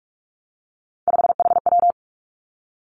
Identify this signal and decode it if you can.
Morse code, heard as a keyed tone, about 46 wpm, 700 Hz. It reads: 5HW
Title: Three Challenging Letter-Number Combinations 45wpm